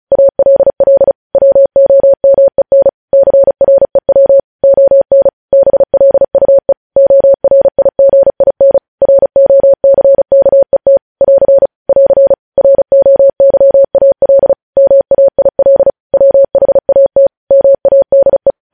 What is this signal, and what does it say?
Morse code, heard as a keyed tone, about 36 wpm, 550 Hz. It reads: ALL WOMEN CREW ON BLUE ORIGIN ROCKET <AR> <AR> ROYAL MAIL WHAT MADE
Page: news headlines in morse code at 35 wpm